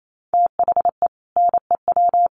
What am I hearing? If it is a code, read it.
Morse code, heard as a keyed tone, about 28 wpm, 700 Hz. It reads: THE NEW
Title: Sets of 3 Words 28wpm